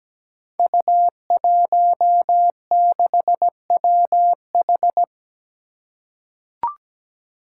Morse code, heard as a keyed tone, about 17 wpm, 700 Hz. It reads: U16WH
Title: Five Challenging Letter-Number Combinations 17wpm